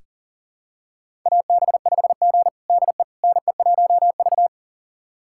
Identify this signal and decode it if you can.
Morse code, heard as a keyed tone, about 40 wpm, 700 Hz. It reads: AB5G DE NE1V
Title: USA Callsign DE Callsign - 4 Characters - 40wpm